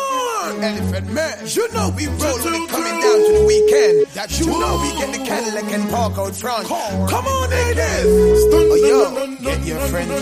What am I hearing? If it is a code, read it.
Ringback tone (Europe, 425 Hz), with speech.